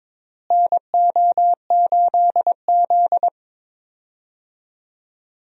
Morse code, heard as a keyed tone, about 21 wpm, 700 Hz. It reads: NO8Z
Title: USA Call Signs - 4 Characters - Repeated 2x - 22wpm